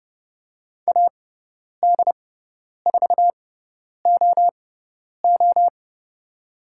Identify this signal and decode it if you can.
Morse code, heard as a keyed tone, about 30 wpm, spaced as for 10 wpm, 690 Hz. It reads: AD4OO
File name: Lesson 18g - Letter m - Call Signs 30wpm